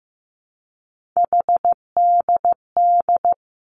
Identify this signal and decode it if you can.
Morse code, heard as a keyed tone, about 15 wpm, 700 Hz. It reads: HDD
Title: Three Challenging Letter-Number Combinations 15wpm